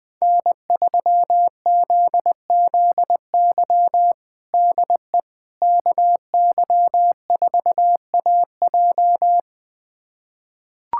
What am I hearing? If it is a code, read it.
Morse code, heard as a keyed tone, about 20 wpm, 700 Hz. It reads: N3ZZY DE KY4AJ
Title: USA Callsign DE Callsign - 5 Characters - 20wpm